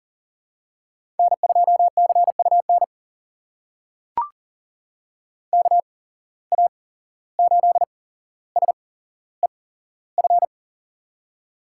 Morse code, heard as a keyed tone, about 40 wpm, spaced as for 16 wpm, 700 Hz. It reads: N2CUN KA8SEF
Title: USA Call Signs - 8x Character Spacing - 40wpm